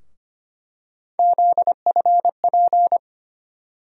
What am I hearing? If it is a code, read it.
Morse code, heard as a keyed tone, about 25 wpm, 700 Hz. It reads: ZFP
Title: USA Call Sign Postfixes - 3 Letters 25wpm